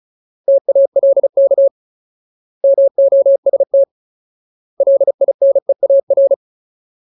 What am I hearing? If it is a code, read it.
Morse code, heard as a keyed tone, about 34 wpm, 550 Hz. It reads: TALK MOST LINEAR
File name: Ultimate Word List / 35WPM Ultimate